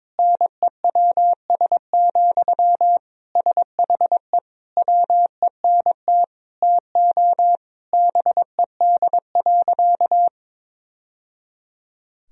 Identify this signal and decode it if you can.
Morse code, heard as a keyed tone, about 22 wpm, 700 Hz. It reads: NEWS, SHE WENT TO BED.
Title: Sentences from Top 1000 Words - Review Part 3 - 22wpm